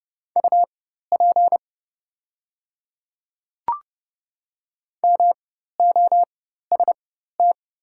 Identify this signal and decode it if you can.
Morse code, heard as a keyed tone, about 30 wpm, spaced as for 14 wpm, 700 Hz. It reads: UP MOST
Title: 4x Character Spacing - Top 100 Words - 30wpm